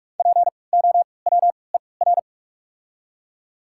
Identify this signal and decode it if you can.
Morse code, heard as a keyed tone, about 48 wpm, spaced as for 26 wpm, 700 Hz. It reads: POWER
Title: 3x Character Spacing - Top 500 Words - 45wpm